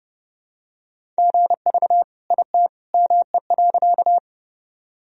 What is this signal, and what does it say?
Morse code, heard as a keyed tone, about 30 wpm, 700 Hz. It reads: GV IT ME.
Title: Sentences from Top 200 Words and Abbreviations 30wpm